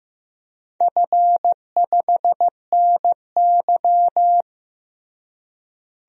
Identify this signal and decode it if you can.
Morse code, heard as a keyed tone, about 15 wpm, 700 Hz. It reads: F5NY
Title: DXCC-100 Call Signs - 4 Characters - Repeated 2x - 15wpm